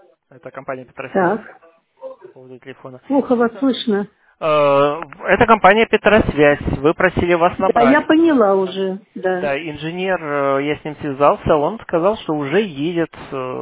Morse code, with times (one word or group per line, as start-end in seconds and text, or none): none